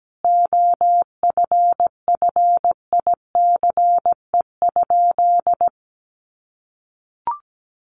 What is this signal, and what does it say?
Morse code, heard as a keyed tone, about 17 wpm, 700 Hz. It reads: OFFICE?
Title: Sentences from Top 1000 Words 17wpm